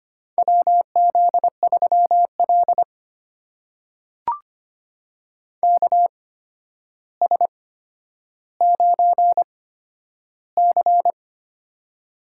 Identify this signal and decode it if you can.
Morse code, heard as a keyed tone, about 25 wpm, spaced as for 11 wpm, 700 Hz. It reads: WZ3L KS9C
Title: USA Call Signs - 4 Characters - 8x Character Spacing - 25wpm